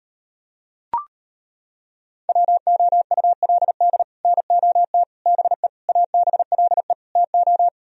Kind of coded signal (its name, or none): Morse code